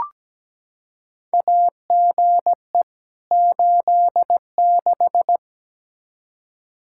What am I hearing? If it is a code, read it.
Morse code, heard as a keyed tone, about 17 wpm, 700 Hz. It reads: AGE 86
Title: K1USN SST QSO Practice 17wpm